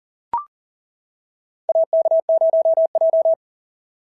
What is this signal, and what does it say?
Morse code, heard as a keyed tone, about 40 wpm, 690 Hz. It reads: AK0J
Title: SOTA QSO Practice 40wpm